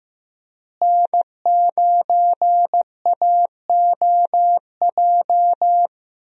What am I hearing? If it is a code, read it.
Morse code, heard as a keyed tone, about 15 wpm, 700 Hz. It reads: N9AOJ
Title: USA Call Signs - 5 Characters - Repeated 3x - 15wpm